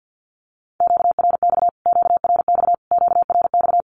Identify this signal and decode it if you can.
Morse code, heard as a keyed tone, about 49 wpm, 700 Hz. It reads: QRX QRX QRX
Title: Q-Codes Repeated 3x 50wpm